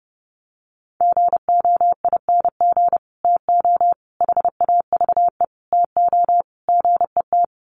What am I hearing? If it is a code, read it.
Morse code, heard as a keyed tone, about 30 wpm, 700 Hz. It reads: GOING TO HAVE TO GET
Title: Sets of 5 Words Encore2 30wpm